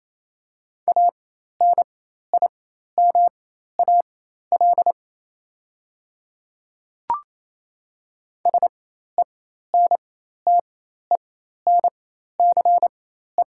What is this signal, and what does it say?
Morse code, heard as a keyed tone, about 28 wpm, spaced as for 13 wpm, 700 Hz. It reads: ANIMAL SENTENCE
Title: 4x Character Spacing - Top 200 Words - 28wpm